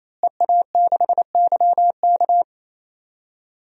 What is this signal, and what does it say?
Morse code, heard as a keyed tone, about 28 wpm, 700 Hz. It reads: EA6YK
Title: DXCC-100 Call Signs - 5 Characters - Repeated 3x - 28wpm